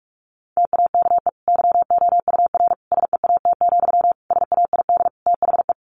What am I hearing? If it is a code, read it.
Morse code, heard as a keyed tone, about 44 wpm, 700 Hz. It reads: TAKE YOUR SEAT, SAID THE
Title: Sentences from Top 1500 Words 45wpm